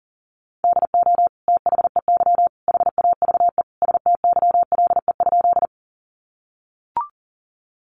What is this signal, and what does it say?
Morse code, heard as a keyed tone, about 40 wpm, 700 Hz. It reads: DO THEY HAVE STYLE?
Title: Sentences from Top 800 Words 40wpm